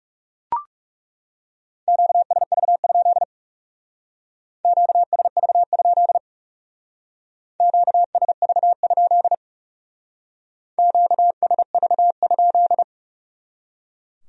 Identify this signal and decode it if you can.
Morse code, heard as a keyed tone, about 38 wpm, 700 Hz. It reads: QSV? QSV? QSV? QSV?